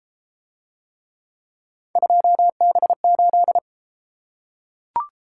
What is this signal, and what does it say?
Morse code, heard as a keyed tone, about 31 wpm, 700 Hz. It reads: 2B8